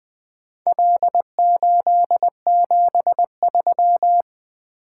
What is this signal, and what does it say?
Morse code, heard as a keyed tone, about 20 wpm, 700 Hz. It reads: L873
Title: Four Letter-Number Combinations - 20wpm